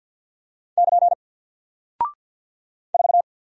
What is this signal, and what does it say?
Morse code, heard as a keyed tone, about 48 wpm, 700 Hz. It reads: <KN> 4
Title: Single Character (Letter:Number:Punctuation:Prosign) Alt 50wpm